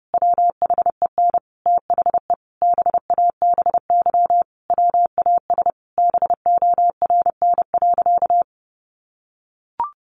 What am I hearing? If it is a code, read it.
Morse code, heard as a keyed tone, about 30 wpm, 700 Hz. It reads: WHEN THE BABY WAS BORN.
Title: Sentences from Top 1300 Words 30wpm